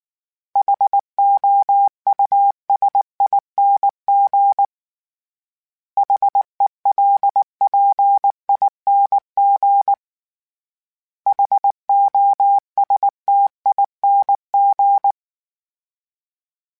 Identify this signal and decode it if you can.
Morse code, heard as a keyed tone, about 19 wpm, 800 Hz. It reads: HOUSING HELPING HOSTING
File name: Special INGs / Special INGs-mp3s / Sngl Ing @18